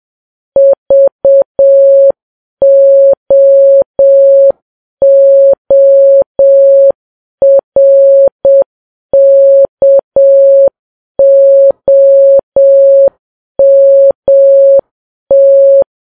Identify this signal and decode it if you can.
Morse code, heard as a keyed tone, about 7 wpm, 550 Hz. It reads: VOORKOMT
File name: Files Web tls / Morse on9fv / morse7kps